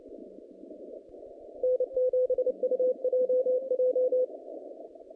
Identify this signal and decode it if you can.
Morse code, heard as a keyed tone, about 29 wpm, 510 Hz. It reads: N7UJJ
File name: contests / naqp-cw-k5tr-op-k5pi- -0000